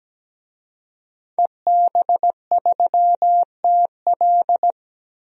Morse code, heard as a keyed tone, about 17 wpm, 700 Hz. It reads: EB3TL
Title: DXCC-100 Call Signs - 5 Characters - 17wpm